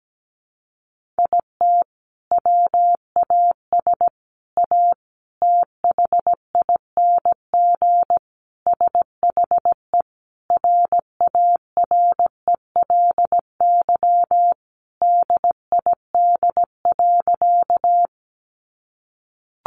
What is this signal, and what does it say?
Morse code, heard as a keyed tone, about 17 wpm, 700 Hz. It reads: IT WAS A THING SHE RARELY DID.
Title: Sentences from Top 1400 Words 17wpm